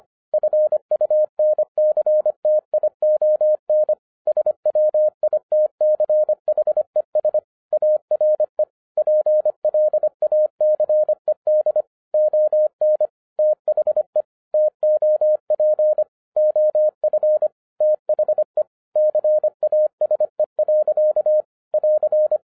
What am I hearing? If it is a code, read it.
Morse code, heard as a keyed tone, about 25 wpm, 610 Hz. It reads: FUNCTION SWITCHES ARE PLACED ON THE TOP OF THE CASE. <AR>
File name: 2011-10-mp3 / n09-CW TO PS2 CONVERTER 25wpm